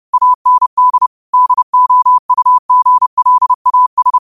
Morse code, heard as a keyed tone, about 30 wpm, 1000 Hz. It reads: AND DOUGLAS
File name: qotdfiles / 30wpm